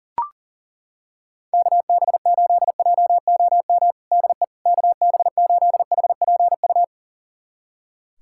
Morse code, heard as a keyed tone, about 40 wpm, 700 Hz. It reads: KB8JOM DE KB8HPU
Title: USA Callsign DE Callsign - 6 Characters - 40wpm